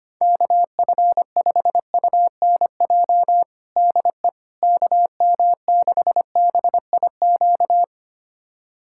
Morse code, heard as a keyed tone, about 25 wpm, 700 Hz. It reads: KF5UNJ DE KM6BIQ
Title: USA Callsign DE Callsign 25wpm